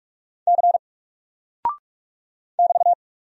Morse code, heard as a keyed tone, about 44 wpm, 700 Hz. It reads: C <BT>
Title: Single Character (Letter:Number:Punctuation:Prosign) Alt 45wpm